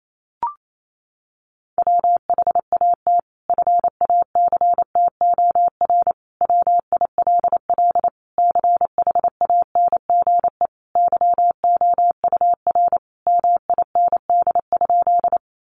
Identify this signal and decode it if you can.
Morse code, heard as a keyed tone, about 28 wpm, 700 Hz. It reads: WHAT FACTOR WILL CHANGE YOUR MIND?
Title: Sentences from Top 1300 Words 28wpm